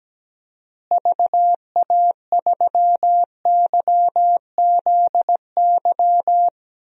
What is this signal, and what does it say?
Morse code, heard as a keyed tone, about 17 wpm, 700 Hz. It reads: VA3YZY
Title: DXCC-100 Call Signs - 6 Characters - Repeated 2x - 17wpm